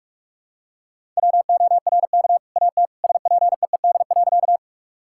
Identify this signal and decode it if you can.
Morse code, heard as a keyed tone, about 46 wpm, 700 Hz. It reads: WORK AT SPEED.
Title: Sentences from Top 1000 Words - Review Part 5 - 45wpm